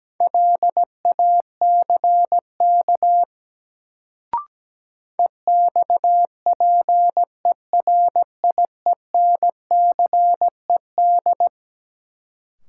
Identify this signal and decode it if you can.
Morse code, heard as a keyed tone, about 17 wpm, 700 Hz. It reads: LACK EXPERIENCED
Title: Sentences from Top 1100 Words - New Word Drill Alt - 17wpm